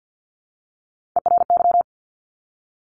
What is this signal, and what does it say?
Morse code, heard as a keyed tone, about 50 wpm, 700 Hz. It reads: ERY
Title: Top Three Letter Combinations in Words 50wpm